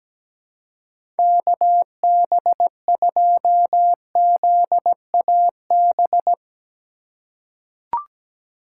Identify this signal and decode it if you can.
Morse code, heard as a keyed tone, about 17 wpm, 700 Hz. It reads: KB2ZAB
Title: USA Call Signs - 2x Character Spacing - 17wpm